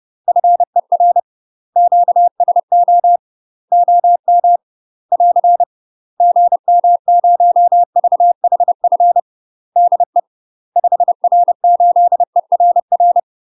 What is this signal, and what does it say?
Morse code, heard as a keyed tone, about 30 wpm, 700 Hz. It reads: FER QSO OM <AR> GM0VHF DE 5R8ERR
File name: UK MASTER QSOs TRAINING/30WPM